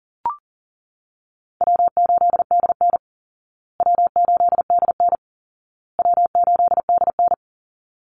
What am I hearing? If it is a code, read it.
Morse code, heard as a keyed tone, about 40 wpm, 700 Hz. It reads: W8DN W8DN W8DN
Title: Easy Call Signs Repeated 3x - 40wpm